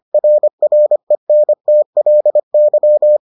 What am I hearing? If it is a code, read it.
Morse code, heard as a keyed tone, about 25 wpm, 600 Hz. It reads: RRENTLY